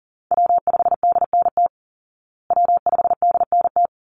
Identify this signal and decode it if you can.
Morse code, heard as a keyed tone, about 40 wpm, 700 Hz. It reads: W5DNT W5DNT
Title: Call Signs Repeated 3x - 40wpm